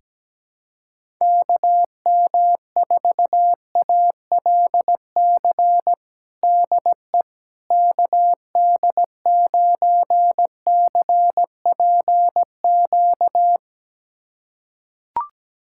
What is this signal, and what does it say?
Morse code, heard as a keyed tone, about 17 wpm, 700 Hz. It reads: KM4ALC DE KD9CPQ
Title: USA Callsign DE Callsign 17wpm